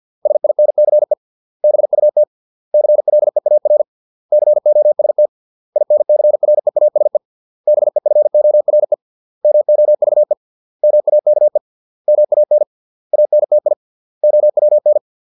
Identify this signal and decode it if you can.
Morse code, heard as a keyed tone, about 50 wpm, 600 Hz. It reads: SINCE BUT CLEAR COST INCREASE BEFORE MOVE MAKE MAN ANTI OWN